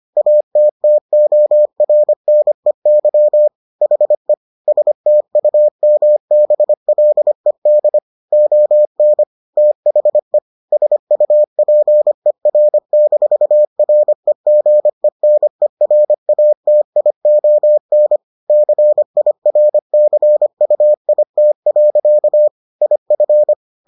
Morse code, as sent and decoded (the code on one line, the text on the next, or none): .- - - --- .-. -. -.-- / .... / ... - ..- -- -... .-.. -.. / --- -. / - .... / ... ..- .--. .-. -....- .-. --. -. .-. .- - .. --- -. / -.-. .. .-. -.-. ..- .. - .-.-.- / .. ..-.
ATTORNEY HE STUMBLED ON THE SUPER-REGENERATION CIRCUIT. IF